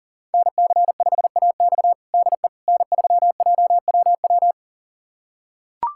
Morse code, heard as a keyed tone, about 40 wpm, 700 Hz. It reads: NC5AX DE N3JWW
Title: USA Callsign DE Callsign - 5 Characters - 40wpm